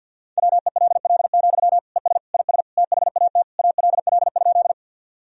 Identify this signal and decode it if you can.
Morse code, heard as a keyed tone, about 50 wpm, 700 Hz. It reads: WELL, ES IS THAT ALL?
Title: Sentences from Top 200 Words and Abbreviations 50wpm